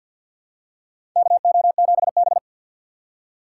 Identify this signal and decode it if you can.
Morse code, heard as a keyed tone, about 52 wpm, 700 Hz. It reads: KO7B
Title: USA Call Signs - 6x Character Spacing - 50wpm